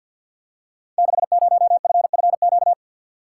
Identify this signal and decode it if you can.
Morse code, heard as a keyed tone, about 46 wpm, 700 Hz. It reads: 60FFQ
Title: Code Groups - 5 Letter:Numbers - 50wpm